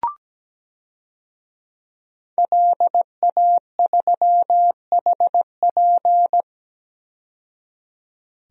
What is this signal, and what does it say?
Morse code, heard as a keyed tone, about 17 wpm, 700 Hz. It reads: LA3HP
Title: DXCC-100 Call Signs - 5 Characters - 17wpm